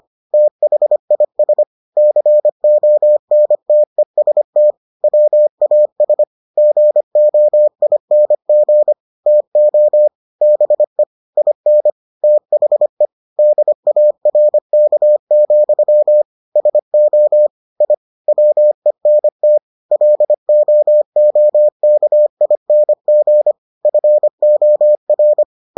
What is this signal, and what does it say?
Morse code, heard as a keyed tone, about 25 wpm, 600 Hz. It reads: THIS CONTEST WAS GOING TO BE IN THE DARK, SO I WENT LOOKING FOR